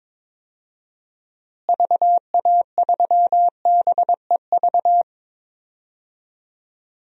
Morse code, heard as a keyed tone, about 21 wpm, 700 Hz. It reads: VA3BEV
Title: DXCC-100 Call Signs - 6 Characters - Repeated 3x - 22wpm